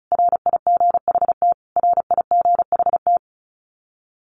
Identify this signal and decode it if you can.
Morse code, heard as a keyed tone, about 34 wpm, 700 Hz. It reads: RIGHT RIGHT
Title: Top 200 Words Repeated 35wpm